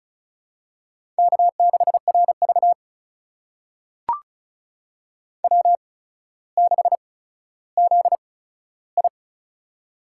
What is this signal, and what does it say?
Morse code, heard as a keyed tone, about 36 wpm, spaced as for 15 wpm, 700 Hz. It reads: K6RV W6ZI
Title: USA Call Signs - 4 Characters - 8x Character Spacing - 35wpm